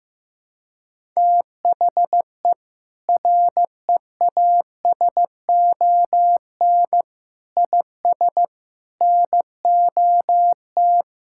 Morse code, heard as a keyed tone, about 15 wpm, 700 Hz. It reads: THE REASON IS NOT
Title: Sentences from Top 1600 Words 15wpm